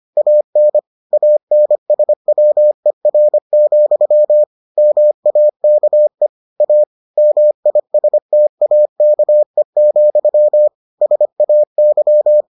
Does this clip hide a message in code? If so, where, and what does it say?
Yes: Morse code from 0.2 to 12.4 s, heard as a keyed tone, about 25 wpm, 600 Hz. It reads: AN ANSWER, MAKE A MISTAKE, SAY